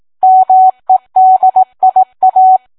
Morse code, heard as a keyed tone, about 18 wpm, 750 Hz. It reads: MEDIA